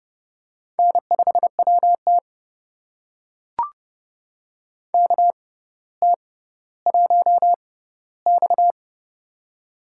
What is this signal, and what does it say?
Morse code, heard as a keyed tone, about 30 wpm, spaced as for 15 wpm, 700 Hz. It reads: N5WT KT1X
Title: USA Call Signs - 4 Characters - 6x Character Spacing - 30wpm